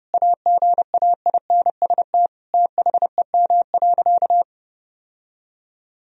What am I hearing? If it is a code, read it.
Morse code, heard as a keyed tone, about 30 wpm, 700 Hz. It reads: AGAINST THEM.
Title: Sentences from Top 1000 Words - Review Part 4 - 30wpm